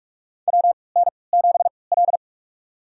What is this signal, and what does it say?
Morse code, heard as a keyed tone, about 46 wpm, spaced as for 26 wpm, 700 Hz. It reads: WN7L
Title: USA Call Signs - 4 Characters - 3x Character Spacing - 45wpm